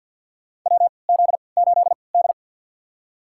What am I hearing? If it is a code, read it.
Morse code, heard as a keyed tone, about 53 wpm, spaced as for 29 wpm, 700 Hz. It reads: WZ8D
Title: USA Call Signs - 4 Characters - 3x Character Spacing - 50wpm